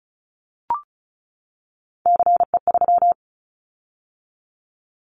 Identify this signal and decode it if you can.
Morse code, heard as a keyed tone, about 34 wpm, 700 Hz. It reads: CE3
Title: Callsign Prefixes for Top 100 DXCC - 35wpm